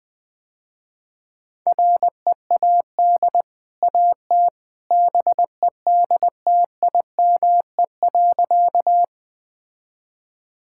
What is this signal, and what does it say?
Morse code, heard as a keyed tone, about 20 wpm, 700 Hz. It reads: READ AT BEDTIME.
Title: Easy Sentences with Hints - 20wpm